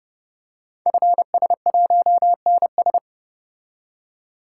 Morse code, heard as a keyed tone, about 30 wpm, 700 Hz. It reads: FS1NS
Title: Code Groups - 5 Letter:Numbers - 30wpm